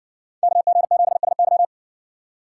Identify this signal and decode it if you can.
Morse code, heard as a keyed tone, about 58 wpm, 700 Hz. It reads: KG7SY